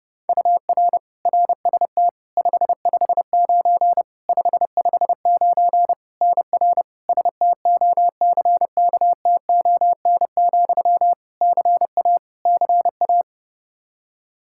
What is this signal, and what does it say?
Morse code, heard as a keyed tone, about 30 wpm, 700 Hz. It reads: UR RST 559 559 NR STOCKTON, CA CA
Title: Practice QSO - Exchange 1 - Beginning - 30wpm